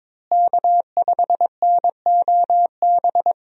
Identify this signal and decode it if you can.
Morse code, heard as a keyed tone, about 22 wpm, 700 Hz. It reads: K5NOB
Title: USA Call Signs - 5 Characters - Repeated 3x - 22wpm